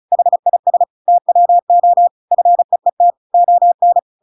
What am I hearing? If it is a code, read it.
Morse code, heard as a keyed tone, about 34 wpm, 700 Hz. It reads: HIS TWO FEET ON